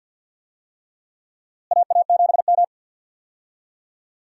Morse code, heard as a keyed tone, about 52 wpm, 700 Hz. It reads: AA7M
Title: USA Call Signs - 4 Characters - Repeated 3x - 50wpm